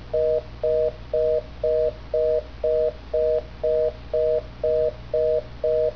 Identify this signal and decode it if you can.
Reorder tone (North America), 480 + 620 Hz, with noise.